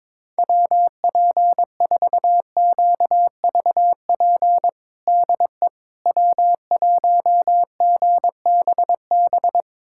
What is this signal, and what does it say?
Morse code, heard as a keyed tone, about 22 wpm, 700 Hz. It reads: WP4QVP DE W1GBB
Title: USA Callsign DE Callsign 22wpm